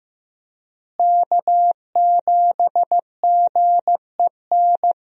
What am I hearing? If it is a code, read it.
Morse code, heard as a keyed tone, about 15 wpm, 700 Hz. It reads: K7GEN
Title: USA Call Signs - 5 Characters - Repeated 3x - 15wpm